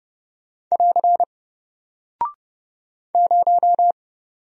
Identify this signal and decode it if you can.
Morse code, heard as a keyed tone, about 30 wpm, 700 Hz. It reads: <AR> 0
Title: Single Character (Letter:Number:Punctuation:Prosign) Alt 30wpm